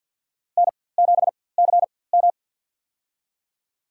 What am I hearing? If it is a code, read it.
Morse code, heard as a keyed tone, about 52 wpm, spaced as for 23 wpm, 700 Hz. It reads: N7CM
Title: USA Call Signs - 4x Character Spacing - 50wpm